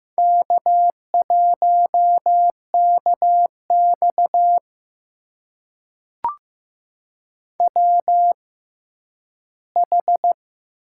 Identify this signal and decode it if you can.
Morse code, heard as a keyed tone, about 15 wpm, spaced as for 10 wpm, 700 Hz. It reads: K1KX WH
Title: USA Call Signs - 4 Characters - 6x Character Spacing - 15wpm